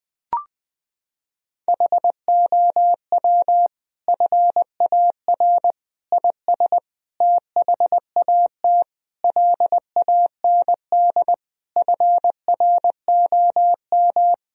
Morse code, heard as a keyed tone, about 20 wpm, 700 Hz. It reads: HOW FAR IS THAT LAND FROM
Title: Sentences from Top 1000 Words - Review Part 1 - 20wpm